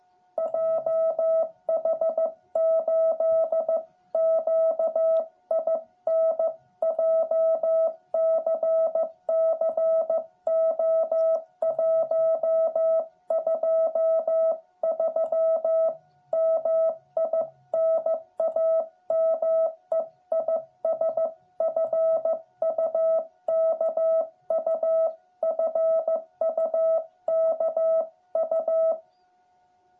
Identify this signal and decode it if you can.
Morse code, heard as a keyed tone, about 15 wpm, 660 Hz. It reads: JH8QIN JCC O123 MI NAME IS FUKU FUKU
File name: jh8cbh / 2021cw